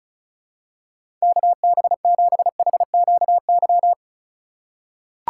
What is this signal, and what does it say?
Morse code, heard as a keyed tone, about 34 wpm, 700 Hz. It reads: KB7HQY
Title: USA Call Signs - 2x Character Spacing - 35wpm